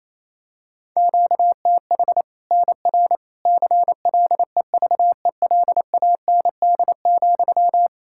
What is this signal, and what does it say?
Morse code, heard as a keyed tone, about 27 wpm, 700 Hz. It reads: QTH NR CLEVELAND,
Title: K1USN SST QSO Practice 28wpm